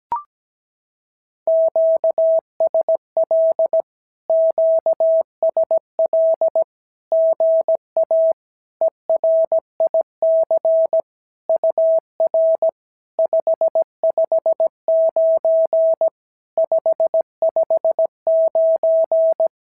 Morse code, heard as a keyed tone, about 17 wpm, 650 Hz. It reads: QSL QSL GA ERIC UR 559 559
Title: SOTA QSO Practice 17wpm